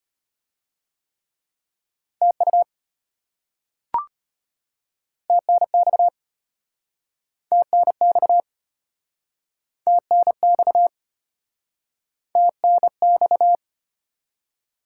Morse code, heard as a keyed tone, about 34 wpm, 700 Hz. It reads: TU TNX TNX TNX TNX